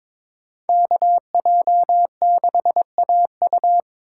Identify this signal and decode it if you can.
Morse code, heard as a keyed tone, about 26 wpm, 700 Hz. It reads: KJ6AU
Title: Speed Racing - USA Call Signs 22wpm